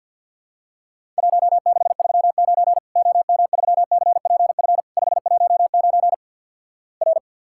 Morse code, heard as a keyed tone, about 50 wpm, 700 Hz. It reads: 1639 OM3CPF 519 R
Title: Worked All Europe DX Contest - QTC Exchange with Repeats - 50wpm